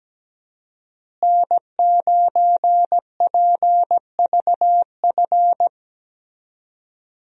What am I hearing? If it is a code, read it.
Morse code, heard as a keyed tone, about 17 wpm, 700 Hz. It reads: N9PVF
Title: USA Call Signs - 5 Characters - Repeated 2x - 17wpm